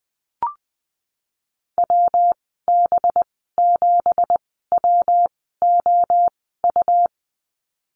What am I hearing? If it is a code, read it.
Morse code, heard as a keyed tone, about 20 wpm, spaced as for 15 wpm, 700 Hz. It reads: WB7WOU
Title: 186 Lesson 30g - Letter b - Call Signs 20wpm